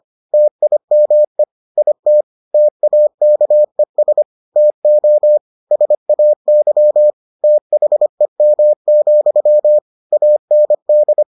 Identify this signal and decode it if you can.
Morse code, heard as a keyed tone, about 25 wpm, 600 Hz. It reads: TIME IT TAKES TO SAY THEM, AND